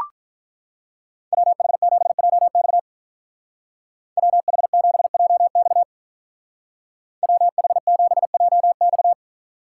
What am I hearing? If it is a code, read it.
Morse code, heard as a keyed tone, about 44 wpm, 700 Hz. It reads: WH7JX WH7JX WH7JX